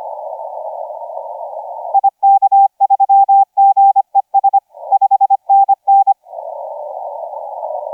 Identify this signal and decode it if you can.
Morse code, heard as a keyed tone, about 25 wpm, 790 Hz. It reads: IK3GES 5NN
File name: SJ0SOP 1650UTC 15m